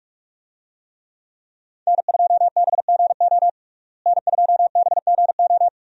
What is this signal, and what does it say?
Morse code, heard as a keyed tone, about 46 wpm, 700 Hz. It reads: N2BGO N2BGO
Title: USA Call Signs - 5 Characters - Repeated 3x - 45wpm